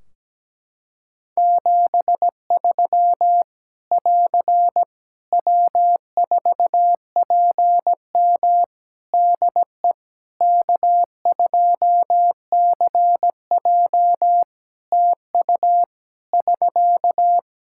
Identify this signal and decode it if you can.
Morse code, heard as a keyed tone, about 17 wpm, 700 Hz. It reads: 73 <AR> W4PM DE K2CJ TU <SK>
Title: K1USN SST QSO Practice 17wpm